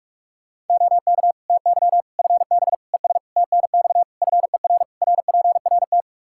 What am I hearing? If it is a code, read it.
Morse code, heard as a keyed tone, about 46 wpm, 700 Hz. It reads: OK TY FB ES TNX FER RPRT
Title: Practice QSO - Exchange 2 - Info - 45wpm